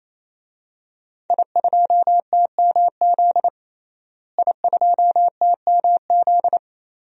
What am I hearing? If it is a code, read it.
Morse code, heard as a keyed tone, about 28 wpm, 690 Hz. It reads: I2TMZ I2TMZ
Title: DXCC-100 Call Signs - 5 Characters - Repeated 3x - 28wpm